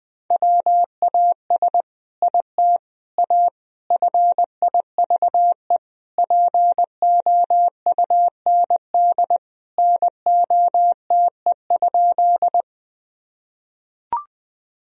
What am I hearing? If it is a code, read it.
Morse code, heard as a keyed tone, about 20 wpm, 700 Hz. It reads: WAS IT A FIVE POUND NOTE?
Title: Sentences from Top 1000 Words - Review Part 2 - 20wpm